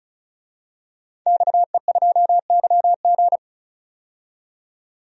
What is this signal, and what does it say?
Morse code, heard as a keyed tone, about 34 wpm, 700 Hz. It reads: XE2YG
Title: DXCC-100 Call Signs - 5 Characters - 35wpm